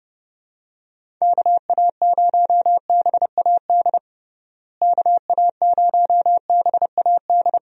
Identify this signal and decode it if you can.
Morse code, heard as a keyed tone, about 32 wpm, 700 Hz. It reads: KA0BAD KA0BAD
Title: USA Call Signs - 6 Characters - Repeated 2x - 30wpm